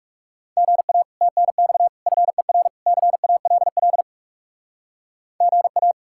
Morse code, heard as a keyed tone, about 38 wpm, 700 Hz. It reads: GA TNX FER CALL GA